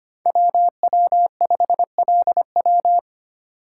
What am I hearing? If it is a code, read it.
Morse code, heard as a keyed tone, about 25 wpm, 700 Hz. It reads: WW5LW
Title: USA Call Signs - 5 Characters - 4x Character Spacing - 25wpm